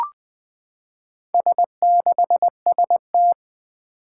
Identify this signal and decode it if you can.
Morse code, heard as a keyed tone, about 20 wpm, 700 Hz. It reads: S6ST
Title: Four Challenging Letter-Number Combinations 20wpm